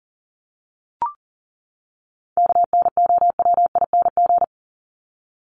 Morse code, heard as a keyed tone, about 40 wpm, 700 Hz. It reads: KNOWING
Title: ING Suffix - Top 100 Words with Base Words 40wpm